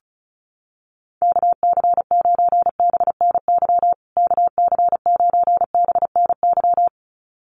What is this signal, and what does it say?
Morse code, heard as a keyed tone, about 34 wpm, 700 Hz. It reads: KC9BNY KC9BNY
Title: USA Call Signs Repeated 2x 35wpm